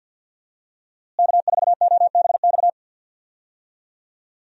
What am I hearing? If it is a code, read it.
Morse code, heard as a keyed tone, about 51 wpm, 700 Hz. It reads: K4OBX
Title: USA Call Signs - 5 Characters - Repeated 2x - 50wpm